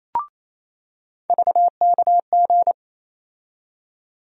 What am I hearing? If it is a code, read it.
Morse code, heard as a keyed tone, about 28 wpm, 700 Hz. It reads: VKG
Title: USA Call Sign Postfixes - 3 Letters 28wpm